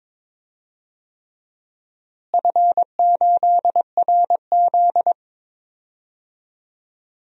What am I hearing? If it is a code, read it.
Morse code, heard as a keyed tone, about 22 wpm, 700 Hz. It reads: F8RZ
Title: DXCC-100 Call Signs - 4 Characters - Repeated 3x - 22wpm